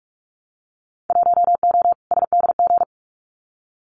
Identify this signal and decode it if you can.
Morse code, heard as a keyed tone, about 46 wpm, 700 Hz. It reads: 1O SDG
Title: Winter Field Day Exchanges Encore 45wpm